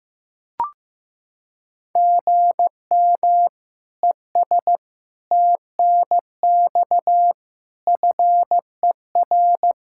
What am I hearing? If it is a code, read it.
Morse code, heard as a keyed tone, about 15 wpm, 700 Hz. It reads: GM ES TNX FER
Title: Practice QSO - Exchange 1 - Beginning - 15wpm